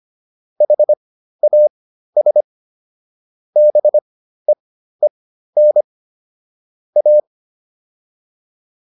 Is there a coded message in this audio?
Yes: Morse code from 0.6 to 7.2 s, heard as a keyed tone, about 25 wpm, spaced as for 13 wpm, 600 Hz. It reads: HAS BEEN A